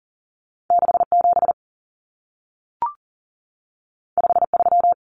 Morse code, heard as a keyed tone, about 40 wpm, 700 Hz. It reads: 67 53
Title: Common Cut Numbers - 2-Digit 40wpm